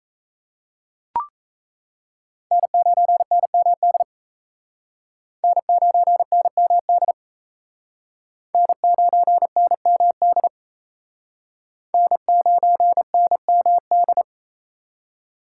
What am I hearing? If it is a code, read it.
Morse code, heard as a keyed tone, about 36 wpm, 700 Hz. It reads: N9NMD N9NMD N9NMD N9NMD